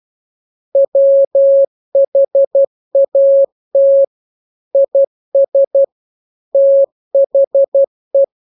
Morse code, heard as a keyed tone, about 12 wpm, 550 Hz. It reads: WHAT IS THE
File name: jokes / joke 15 12 0000